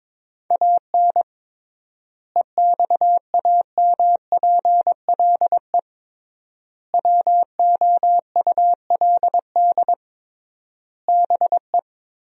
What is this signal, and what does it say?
Morse code, heard as a keyed tone, about 21 wpm, 700 Hz. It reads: AN EXAMPLE WOULD BE
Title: 3x Word-Spacing - Sentences from Top 400 Words - 22wpm